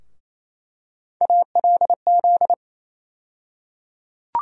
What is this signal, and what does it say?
Morse code, heard as a keyed tone, about 28 wpm, 700 Hz. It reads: ALZ
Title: USA Call Sign Postfixes - 3 Letters 28wpm